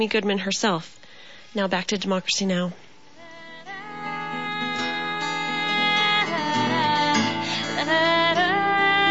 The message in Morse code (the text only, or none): none